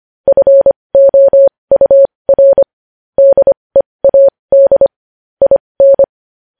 Morse code, heard as a keyed tone, about 25 wpm, spaced as for 20 wpm, 550 Hz. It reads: FOUR DEAD IN
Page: News Headlines in Morse Code at 20 WPM